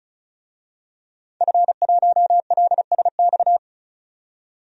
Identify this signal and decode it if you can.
Morse code, heard as a keyed tone, about 36 wpm, 700 Hz. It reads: F1LSX
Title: DXCC-100 Call Signs - 5 Characters - 35wpm